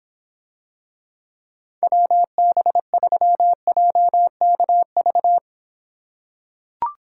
Morse code, heard as a keyed tone, about 25 wpm, 700 Hz. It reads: WB3JKV